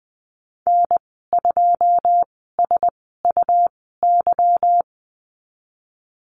Morse code, heard as a keyed tone, about 20 wpm, spaced as for 15 wpm, 700 Hz. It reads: N2SUY
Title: 161 Lesson 26g - Letter g - Call Signs 20wpm